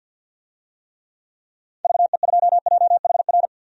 Morse code, heard as a keyed tone, about 50 wpm, 700 Hz. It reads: VE2JHR
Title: DXCC-100 Call Signs - 6 Characters - 50wpm